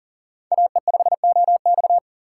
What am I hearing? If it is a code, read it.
Morse code, heard as a keyed tone, about 40 wpm, 700 Hz. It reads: AE5OX
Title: USA Call Signs - 3x Character Spacing - 40wpm